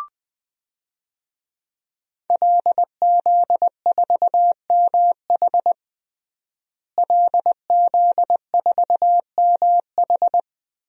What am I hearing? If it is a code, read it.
Morse code, heard as a keyed tone, about 20 wpm, 700 Hz. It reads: LZ4MH LZ4MH
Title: DXCC-100 Call Signs - 5 Characters - Repeated 3x - 20wpm